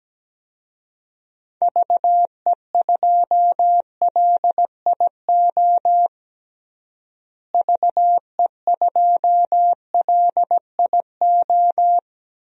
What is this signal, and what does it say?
Morse code, heard as a keyed tone, about 17 wpm, 700 Hz. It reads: VE2LIO VE2LIO
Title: DXCC-100 Call Signs - 6 Characters - Repeated 2x - 17wpm